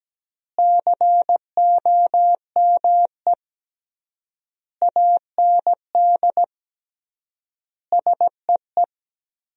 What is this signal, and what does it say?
Morse code, heard as a keyed tone, about 17 wpm, 700 Hz. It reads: COME AND SEE
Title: 3x Word-Spacing - Sentences from Top 100 Words - 17wpm